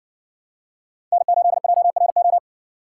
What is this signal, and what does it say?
Morse code, heard as a keyed tone, about 60 wpm, 700 Hz. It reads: N8JRO